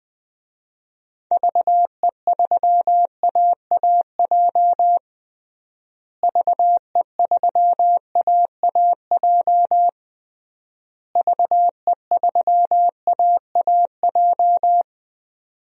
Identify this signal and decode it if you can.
Morse code, heard as a keyed tone, about 20 wpm, 700 Hz. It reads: VE3AAJ VE3AAJ VE3AAJ
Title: DXCC-100 Call Signs - 6 Characters - Repeated 3x - 20wpm